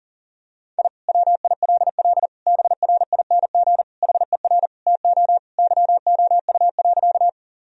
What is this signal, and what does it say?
Morse code, heard as a keyed tone, about 40 wpm, 700 Hz. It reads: I WILL BRING HER TO YOU.
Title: Sentences from Top 1000 Words - Review Part 3 - 40wpm